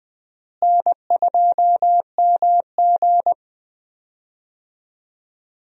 Morse code, heard as a keyed tone, about 19 wpm, 700 Hz. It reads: N2MG
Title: USA Call Signs - 4 Characters - Repeated 3x - 20wpm